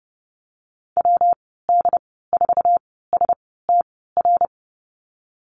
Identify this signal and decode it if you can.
Morse code, heard as a keyed tone, about 30 wpm, 700 Hz. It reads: W D 4 S T R
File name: Lesson 19g - Letter w - Call Signs 30wpm